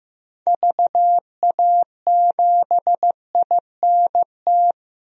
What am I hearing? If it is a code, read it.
Morse code, heard as a keyed tone, about 15 wpm, 700 Hz. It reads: VA7INT
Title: DXCC-100 Call Signs - 6 Characters - Repeated 3x - 15wpm